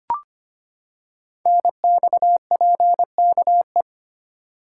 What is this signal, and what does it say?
Morse code, heard as a keyed tone, about 25 wpm, 700 Hz. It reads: NXPKE
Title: Code Groups - 5 Letters - 25wpm